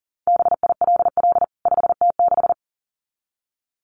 Morse code, heard as a keyed tone, about 40 wpm, 700 Hz. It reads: BILL 5T6
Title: CWT Exchange 40wpm